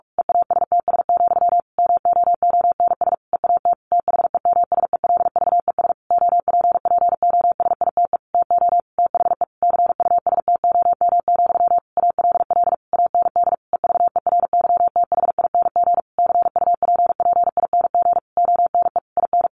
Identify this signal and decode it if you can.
Morse code, heard as a keyed tone, about 46 wpm, 700 Hz. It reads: EASTS, MOONS EAT THEMSELVES OPPOSITE TO THE CUSTOM, ALL AND EVERYTHING CUPPING ONE IN
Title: A Sound Of Thunder 45wpm